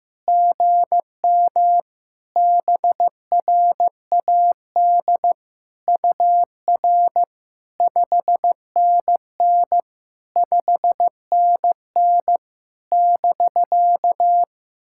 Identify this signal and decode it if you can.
Morse code, heard as a keyed tone, about 15 wpm, 700 Hz. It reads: GM BRAD UR 5NN 5NN <BK>
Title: SOTA QSO Practice 15wpm